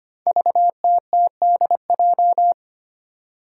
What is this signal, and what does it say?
Morse code, heard as a keyed tone, about 25 wpm, 700 Hz. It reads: VTTDJ
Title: Five Challenging Letter-Number Combinations 25wpm